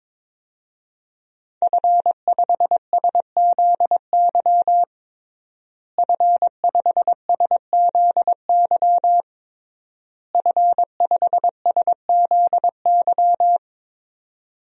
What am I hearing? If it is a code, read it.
Morse code, heard as a keyed tone, about 22 wpm, 700 Hz. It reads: F5SZY F5SZY F5SZY
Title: DXCC-100 Call Signs - 5 Characters - Repeated 3x - 22wpm